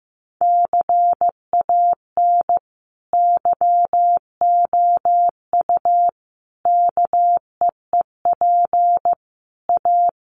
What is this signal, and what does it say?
Morse code, heard as a keyed tone, about 15 wpm, 700 Hz. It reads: CAN YOU KEEP A
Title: Sentences from Top 1200 Words 15wpm